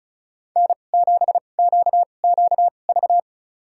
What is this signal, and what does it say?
Morse code, heard as a keyed tone, about 36 wpm, spaced as for 25 wpm, 700 Hz. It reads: N7QQV
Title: USA Call Signs - 5 Characters - 2x Character Spacing - 35wpm